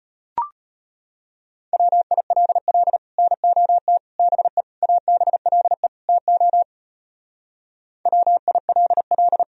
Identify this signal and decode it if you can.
Morse code, heard as a keyed tone, about 33 wpm, 700 Hz. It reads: WILL NOT BE ABLE TO WILL